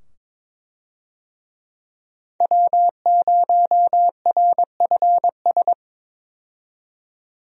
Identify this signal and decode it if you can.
Morse code, heard as a keyed tone, about 22 wpm, 700 Hz. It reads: W0RFS
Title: USA Call Signs - 5 Characters - 22wpm